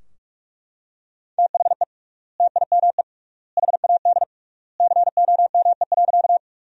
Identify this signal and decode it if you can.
Morse code, heard as a keyed tone, about 41 wpm, 700 Hz. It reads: THE TIME HAD COME.
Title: 3x Word-Spacing - Sentences from Top 100 Words - 45wpm